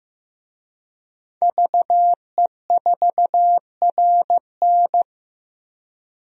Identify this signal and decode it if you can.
Morse code, heard as a keyed tone, about 15 wpm, 700 Hz. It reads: VE4RN
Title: DXCC-100 Call Signs - 5 Characters - Repeated 3x - 15wpm